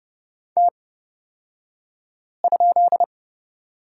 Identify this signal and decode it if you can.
Morse code, heard as a keyed tone, about 29 wpm, spaced as for 5 wpm, 690 Hz. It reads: T?
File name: Lesson 36d - BT - Two Characters 30wpm